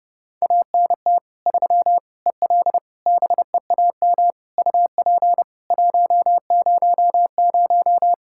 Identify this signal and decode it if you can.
Morse code, heard as a keyed tone, about 30 wpm, 700 Hz. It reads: ANT 3 EL BEAM UP 100
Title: Practice QSO - Exchange 2 - Info - 30wpm